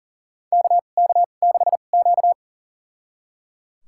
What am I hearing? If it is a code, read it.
Morse code, heard as a keyed tone, about 37 wpm, spaced as for 28 wpm, 700 Hz. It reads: KK6Q
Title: USA Call Signs - 2x Character Spacing - 40wpm